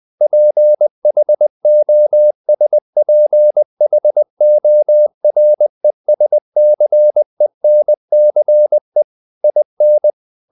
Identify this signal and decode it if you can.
Morse code, heard as a keyed tone, about 20 wpm, 600 Hz. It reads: PHOSPHORESCENCE IN
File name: the old man and the sea / the old man and the sea part 1 20wpm